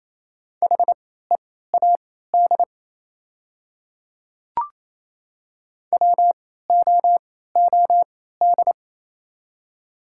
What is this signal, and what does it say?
Morse code, heard as a keyed tone, about 28 wpm, spaced as for 16 wpm, 700 Hz. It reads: HEAD WOOD
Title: 3x Character Spacing - Top 300 Words - 28wpm